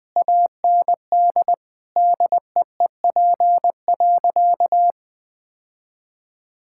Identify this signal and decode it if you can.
Morse code, heard as a keyed tone, about 20 wpm, 700 Hz. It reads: AND DEEP.
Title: Sentences from Top 1600 Words 20wpm